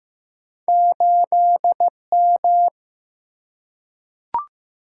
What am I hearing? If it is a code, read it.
Morse code, heard as a keyed tone, about 15 wpm, 700 Hz. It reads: 8 M
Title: Single-Letter Single-Number 15wpm